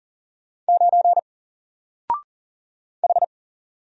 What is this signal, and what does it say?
Morse code, heard as a keyed tone, about 40 wpm, 700 Hz. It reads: 9 H
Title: Single Letter-Number 40wpm